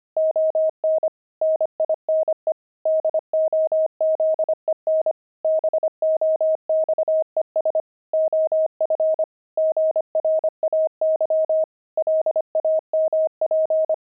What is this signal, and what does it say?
Morse code, heard as a keyed tone, about 25 wpm, 630 Hz. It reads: ON NINE DOZEN BOXES OF GRAY LAMP
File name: pangram-fs-25-cs-25-nine